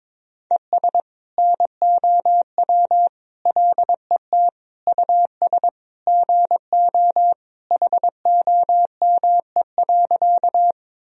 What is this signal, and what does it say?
Morse code, heard as a keyed tone, about 22 wpm, 700 Hz. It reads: ES NOW LET US GO HOME.
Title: Sentences from Top 300 Words and Abbreviations 22wpm